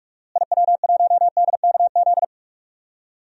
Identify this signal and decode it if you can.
Morse code, heard as a keyed tone, about 41 wpm, 700 Hz. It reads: IW1DKZ
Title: DXCC-100 Call Signs - 6 Characters - Repeated 3x - 45wpm